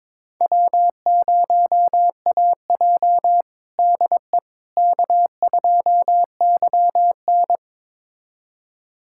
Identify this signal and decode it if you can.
Morse code, heard as a keyed tone, about 22 wpm, 700 Hz. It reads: W0AJ DE K2YN
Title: USA Callsign DE Callsign - 4 Characters - 22wpm